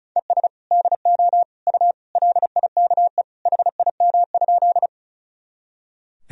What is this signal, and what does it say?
Morse code, heard as a keyed tone, about 34 wpm, 700 Hz. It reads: ES DO U LIKE HIM?
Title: Sentences from Top 100 Words and Abbreviations 35wpm